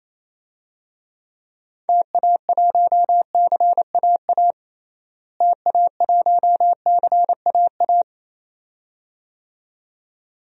Morse code, heard as a keyed tone, about 28 wpm, 700 Hz. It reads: TA1CAA TA1CAA
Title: DXCC-100 Call Signs - 6 Characters - Repeated 2x - 28wpm